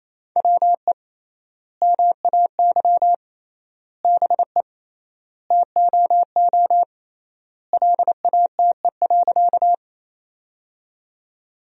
Morse code, heard as a keyed tone, about 28 wpm, 700 Hz. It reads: WE MAY BE TOO LATE.
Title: 3x Word-Spacing - Sentences from Top 300 Words - 28wpm